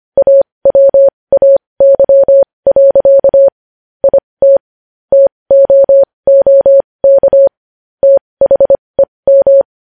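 Morse code, heard as a keyed tone, about 25 wpm, spaced as for 20 wpm, 550 Hz. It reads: AWAY. IT TOOK THEM
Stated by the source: Morse code practice sending news headlines at 20 wpm.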